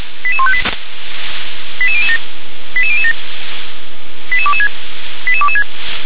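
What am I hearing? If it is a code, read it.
Five-tone selective call (ZVEI), heard as quick runs of five tones, tones of about 70 ms. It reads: 88158 88007 88007 88226 88226
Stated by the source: SDR-received ZVEI five-tone selective call, code 88101.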